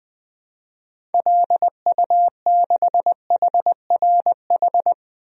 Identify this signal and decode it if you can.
Morse code, heard as a keyed tone, about 20 wpm, 700 Hz. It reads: LU6HRH
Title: DXCC-100 Call Signs - 6 Characters - 20wpm